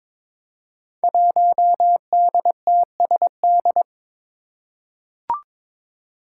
Morse code, heard as a keyed tone, about 22 wpm, 700 Hz. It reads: 1DTSD
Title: Five Challenging Letter-Number Combinations 22wpm